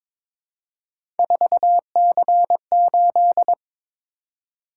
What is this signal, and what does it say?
Morse code, heard as a keyed tone, about 22 wpm, 700 Hz. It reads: 4C8
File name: practice / 22wpm / Three Letter-Number Combinations - 22wpm